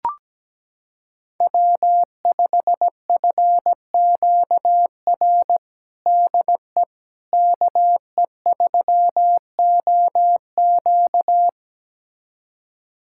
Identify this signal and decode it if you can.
Morse code, heard as a keyed tone, about 17 wpm, 700 Hz. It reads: W5FQR DE KE3OQ
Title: USA Callsign DE Callsign - 5 Characters - 17wpm